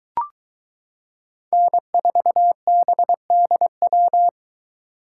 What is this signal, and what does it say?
Morse code, heard as a keyed tone, about 22 wpm, 700 Hz. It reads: N4BDW